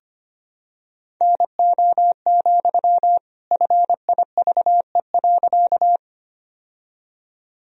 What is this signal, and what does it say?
Morse code, heard as a keyed tone, about 25 wpm, 700 Hz. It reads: NO, FIVE.
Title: Sentences from Top 1000 Words - Review Part 2 - 25wpm